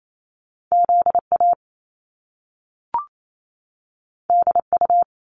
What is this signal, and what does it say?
Morse code, heard as a keyed tone, about 28 wpm, 700 Hz. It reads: ZA DU
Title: Top Two Letter Combinations in Words 28wpm